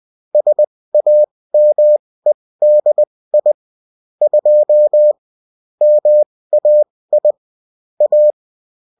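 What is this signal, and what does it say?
Morse code, heard as a keyed tone, about 20 wpm, spaced as for 16 wpm, 600 Hz. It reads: SAMEDI 2 MAI A
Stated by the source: Morse code bulletin of a French amateur CW union, sent as text and in CW.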